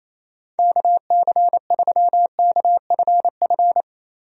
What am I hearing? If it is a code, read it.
Morse code, heard as a keyed tone, about 27 wpm, 700 Hz. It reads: KC3KFF
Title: USA Call Signs - 6 Characters - Repeated 2x - 28wpm